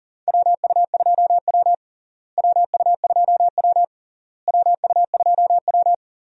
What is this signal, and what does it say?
Morse code, heard as a keyed tone, about 40 wpm, 700 Hz. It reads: WU2W WU2W WU2W
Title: USA Call Signs - 4 Characters - Repeated 3x - 40wpm